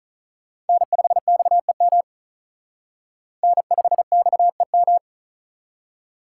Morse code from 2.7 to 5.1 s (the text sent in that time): N5XEM